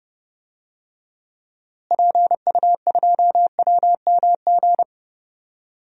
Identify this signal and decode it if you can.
Morse code, heard as a keyed tone, about 30 wpm, 700 Hz. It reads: PU2WMG
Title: DXCC-100 Call Signs - 6 Characters - Repeated 2x - 30wpm